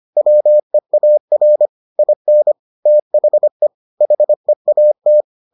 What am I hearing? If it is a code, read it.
Morse code, heard as a keyed tone, about 25 wpm, 600 Hz. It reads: WEAR IN THE HEAT